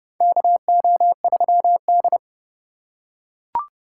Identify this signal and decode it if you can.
Morse code, heard as a keyed tone, about 30 wpm, 700 Hz. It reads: KO3D
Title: USA Call Signs - 4 Characters - 3x Character Spacing - 30wpm